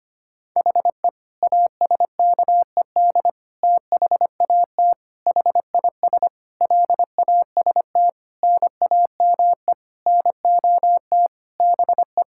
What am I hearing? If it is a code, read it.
Morse code, heard as a keyed tone, about 25 wpm, 700 Hz. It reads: HE ASKED THAT HIS LAST NAME NOT BE
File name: practice / 25wpm / Sentences with 9-Word Phrases - 25wpm